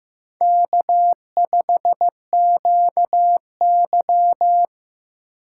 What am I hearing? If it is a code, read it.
Morse code, heard as a keyed tone, about 15 wpm, 700 Hz. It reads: K5QY
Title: USA Call Signs - 4 Characters - 8x Character Spacing - 15wpm